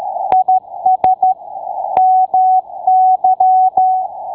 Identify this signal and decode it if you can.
Morse code, heard as a keyed tone, about 13 wpm, 750 Hz. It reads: IS MY